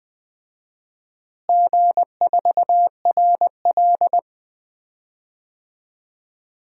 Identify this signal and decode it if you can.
Morse code, heard as a keyed tone, about 20 wpm, 700 Hz. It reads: G4RL
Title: DXCC-100 Call Signs - 4 Characters - Repeated 3x - 20wpm